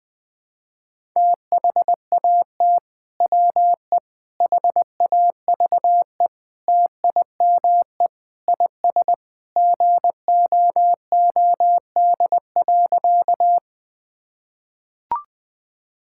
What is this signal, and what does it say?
Morse code, heard as a keyed tone, about 20 wpm, 700 Hz. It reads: THAT WE HAVE TIME IS GOOD.
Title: Sentences with 3-Word Phrases - 20wpm